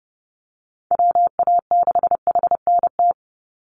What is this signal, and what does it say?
Morse code, heard as a keyed tone, about 30 wpm, 700 Hz. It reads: WA6HNT
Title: USA Call Signs 30wpm